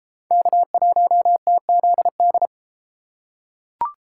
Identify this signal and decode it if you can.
Morse code, heard as a keyed tone, about 30 wpm, 700 Hz. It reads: K1TZD